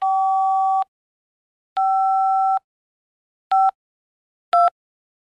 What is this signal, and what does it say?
DTMF, heard as telephone touch tones, tones of about 190 ms. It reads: 5 5 2
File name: telephone / libsounds.com - Tone dialing sounds on the handset